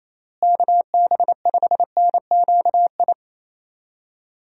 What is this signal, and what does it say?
Morse code, heard as a keyed tone, about 28 wpm, 700 Hz. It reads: KB5NQI
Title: USA Call Signs - 6 Characters - Repeated 2x - 28wpm